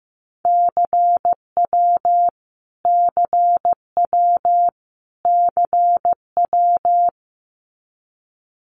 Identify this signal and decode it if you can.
Morse code, heard as a keyed tone, about 15 wpm, 700 Hz. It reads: CW CW CW
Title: Abbreviations Repeated 3x - 15wpm